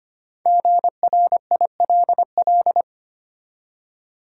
Morse code, heard as a keyed tone, about 25 wpm, 700 Hz. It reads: GRILL
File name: practice / 25wpm / Easy Words with Hints - 25wpm